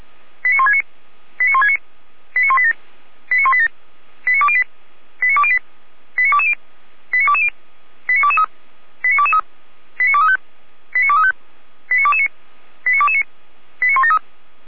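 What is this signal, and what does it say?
Five-tone selective call (ZVEI), heard as quick runs of five tones, tones of about 70 ms. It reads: 89169 89169 89187 89187 89208 89208 89220 89220 89223 89223 89235 89235 89229 89229 89173
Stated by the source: ZVEI five-tone selective call received by SDR.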